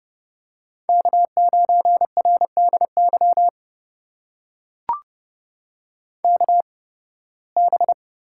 Morse code, heard as a keyed tone, about 30 wpm, spaced as for 20 wpm, 700 Hz. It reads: K9RDY KB
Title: USA Call Signs - 5 Characters - 8x Character Spacing - 30wpm